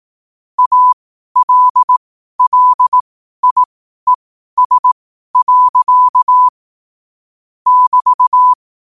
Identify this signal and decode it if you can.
Morse code, heard as a keyed tone, about 18 wpm, spaced as for 12 wpm, 1000 Hz. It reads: ALLIES. <BT>
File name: qotdfiles / 12wpm